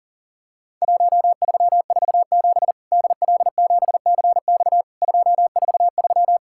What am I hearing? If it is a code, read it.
Morse code, heard as a keyed tone, about 40 wpm, 700 Hz. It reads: 1347 DL7CX 243
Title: Worked All Europe DX Contest - QTC Exchange - 40wpm